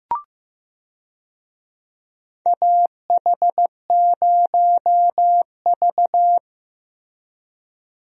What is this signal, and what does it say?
Morse code, heard as a keyed tone, about 15 wpm, 700 Hz. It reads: AH0V
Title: USA Call Signs - 4 Characters - Repeated 3x - 15wpm